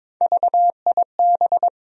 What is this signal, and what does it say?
Morse code, heard as a keyed tone, about 22 wpm, 700 Hz. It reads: VIB
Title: USA Call Sign Postfixes - 3 Letters 22wpm